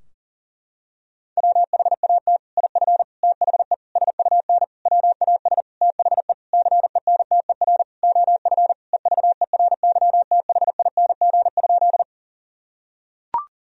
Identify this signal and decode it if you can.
Morse code, heard as a keyed tone, about 40 wpm, 700 Hz. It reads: WHAT IF THE SUN WAS THE CENTER OF EVERYTHING?
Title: Sentences from Top 1000 Words - Review Part 2 - 40wpm